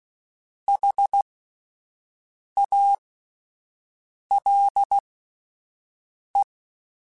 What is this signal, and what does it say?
Morse code, heard as a keyed tone, about 16 wpm, spaced as for 5 wpm, 800 Hz. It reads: HALE